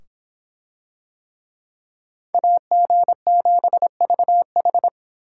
Morse code, heard as a keyed tone, about 26 wpm, 700 Hz. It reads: AG7VH